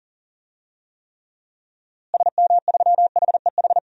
Morse code, heard as a keyed tone, about 40 wpm, 700 Hz. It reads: SM3HEH
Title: DXCC-100 Call Signs - 6 Characters - Repeated 2x - 40wpm